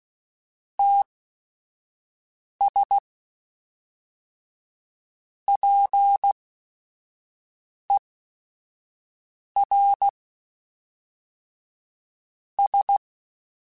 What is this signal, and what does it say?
Morse code, heard as a keyed tone, about 16 wpm, spaced as for 4 wpm, 800 Hz. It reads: TS PERS
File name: Technical Subjects / Technical Subjects-mp3s / XL @5